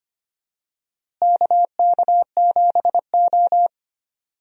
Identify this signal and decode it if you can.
Morse code, heard as a keyed tone, about 25 wpm, 700 Hz. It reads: KK7O
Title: USA Call Signs - 4 Characters - Repeated 3x - 25wpm